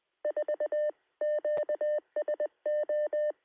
Morse code, heard as a keyed tone, about 19 wpm, 560 Hz. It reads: 4 QSO
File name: sm7lcb / 2139z oh3lwp 3cm